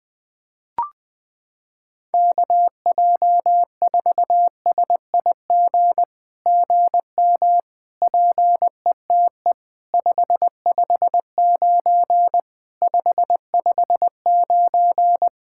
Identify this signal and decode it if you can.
Morse code, heard as a keyed tone, about 20 wpm, 700 Hz. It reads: KJ4SIG GM PETE 559 559
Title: SOTA QSO Practice 20wpm